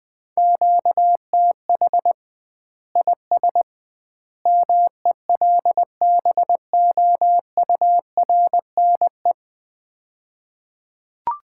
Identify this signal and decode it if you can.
Morse code, heard as a keyed tone, about 20 wpm, 700 Hz. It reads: QTH IS MELBOURNE
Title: Common QSO Phrases - 2x Word-Spacing - 20wpm